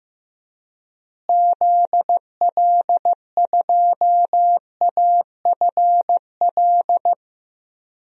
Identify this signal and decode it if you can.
Morse code, heard as a keyed tone, about 15 wpm, 700 Hz. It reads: ZL2AFL
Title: DXCC-100 Call Signs - 6 Characters - Repeated 2x - 15wpm